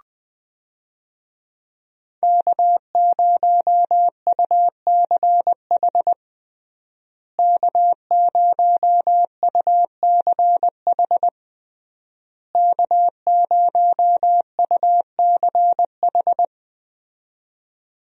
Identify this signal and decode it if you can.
Morse code, heard as a keyed tone, about 20 wpm, 700 Hz. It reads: K0UCH K0UCH K0UCH
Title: USA Call Signs - 5 Characters - Repeated 3x - 20wpm